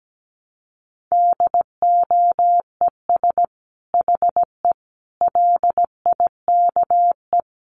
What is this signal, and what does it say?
Morse code, heard as a keyed tone, about 17 wpm, 700 Hz. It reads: DOES HE LIKE
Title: Sentences from Top 600 Words 17wpm